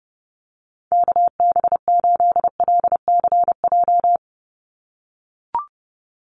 Morse code, heard as a keyed tone, about 29 wpm, 700 Hz. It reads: KB8LCJ
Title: USA Call Signs 30wpm